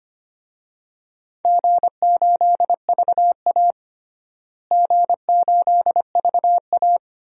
Morse code, heard as a keyed tone, about 25 wpm, 700 Hz. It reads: G8VA G8VA
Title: DXCC-100 Call Signs - 4 Characters - Repeated 3x - 25wpm